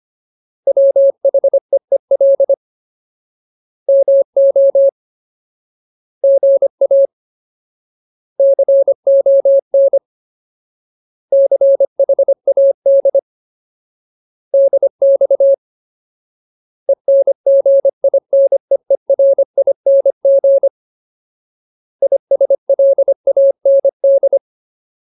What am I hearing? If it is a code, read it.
Morse code, heard as a keyed tone, about 25 wpm, 550 Hz. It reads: WHEEL MO GA CON CHAD DX ENGINEERING ISLAND
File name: Ultimate Word List / 25WPM Ultimate